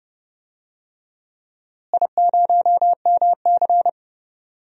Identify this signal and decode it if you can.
Morse code, heard as a keyed tone, about 32 wpm, 700 Hz. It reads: I0MC
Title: DXCC-100 Call Signs - 4 Characters - Repeated 3x - 30wpm